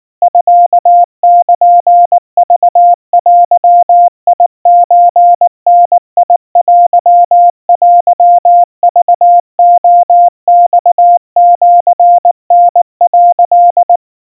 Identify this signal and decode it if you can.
Morse code (Japanese Wabun code), heard as a keyed tone, about 19 wpm, 700 Hz. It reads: ミルクデソダテテクレマシタ」
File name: kawa / Morse sample / kamyu 70 cpm 700 Hz Crispy